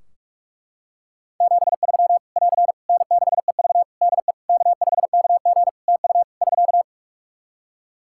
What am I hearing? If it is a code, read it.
Morse code, heard as a keyed tone, about 43 wpm, 700 Hz. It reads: 73 <AR> N6EV DE K5KG TU <SK>
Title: K1USN SST QSO Practice 45wpm